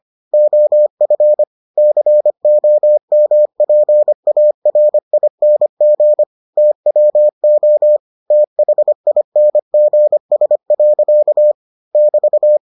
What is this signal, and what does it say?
Morse code, heard as a keyed tone, about 25 wpm, 600 Hz. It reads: OF COMPARING TWO THINGS. <BT>